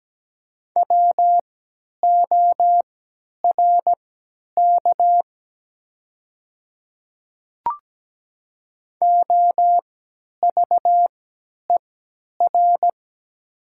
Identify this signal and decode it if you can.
Morse code, heard as a keyed tone, about 17 wpm, spaced as for 10 wpm, 700 Hz. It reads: WORK OVER
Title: 3x Character Spacing - Top 200 Words - 17wpm